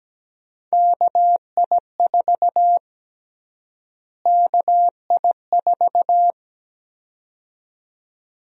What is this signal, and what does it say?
Morse code, heard as a keyed tone, about 17 wpm, 700 Hz. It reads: KI4 KI4
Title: USA Call Sign Prefixes - Repeated 2x - 17wpm